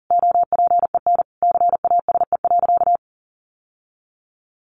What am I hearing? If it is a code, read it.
Morse code, heard as a keyed tone, about 40 wpm, 700 Hz. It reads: OPEN CASE.
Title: Sentences from Top 1000 Words 40wpm